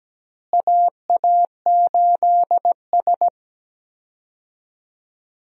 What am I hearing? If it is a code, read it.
Morse code, heard as a keyed tone, about 17 wpm, 700 Hz. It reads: AA8S
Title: USA Call Signs - 4 Characters - Repeated 3x - 17wpm